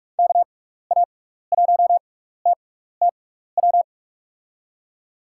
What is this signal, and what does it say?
Morse code, heard as a keyed tone, about 45 wpm, spaced as for 15 wpm, 700 Hz. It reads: KA1TTW
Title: USA Call Signs - 6 Characters - 6x Character Spacing - 45wpm